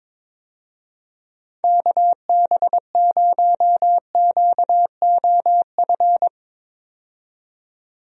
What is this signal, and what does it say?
Morse code, heard as a keyed tone, about 21 wpm, 700 Hz. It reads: KB0QOF
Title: USA Call Signs - 6 Characters - 22wpm